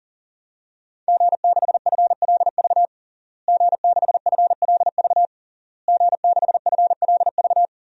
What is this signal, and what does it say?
Morse code, heard as a keyed tone, about 40 wpm, 700 Hz. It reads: G6FLV G6FLV G6FLV
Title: DXCC-100 Call Signs - 5 Characters - Repeated 3x - 40wpm